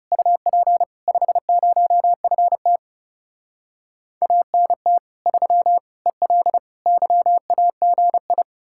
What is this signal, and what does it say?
Morse code, heard as a keyed tone, about 40 wpm, 700 Hz. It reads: UP 50FT ANT 3 EL YAGI